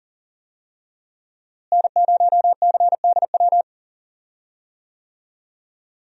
Morse code, heard as a keyed tone, about 40 wpm, 700 Hz. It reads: N0CDW
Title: USA Call Signs - 5 Characters - Repeated 2x - 40wpm